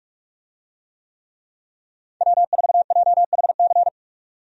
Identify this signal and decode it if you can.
Morse code, heard as a keyed tone, about 43 wpm, 700 Hz. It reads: W4JHC
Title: USA Call Signs - 5 Characters - 45wpm